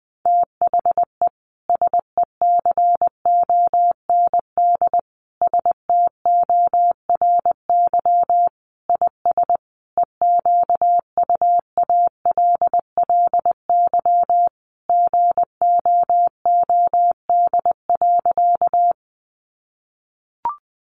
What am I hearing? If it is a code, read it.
Morse code, heard as a keyed tone, about 20 wpm, 700 Hz. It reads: THE SECOND STORY IS EQUALLY GOOD.
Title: Sentences from Top 1400 Words 20wpm